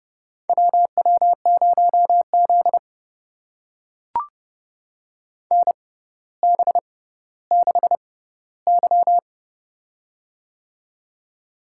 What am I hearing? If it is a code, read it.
Morse code, heard as a keyed tone, about 31 wpm, spaced as for 15 wpm, 700 Hz. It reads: WW0Z NB6Y
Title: USA Call Signs - 4 Characters - 6x Character Spacing - 30wpm